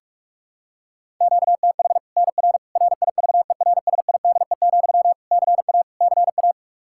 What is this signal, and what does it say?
Morse code, heard as a keyed tone, about 46 wpm, 700 Hz. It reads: QTH NR RIVERSIDE, CA CA
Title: Practice QSO - Exchange 1 - Beginning - 45wpm